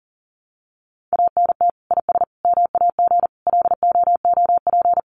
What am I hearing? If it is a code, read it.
Morse code, heard as a keyed tone, about 40 wpm, 700 Hz. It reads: ANT IS MAG LOOP
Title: Common QSO Phrases 40wpm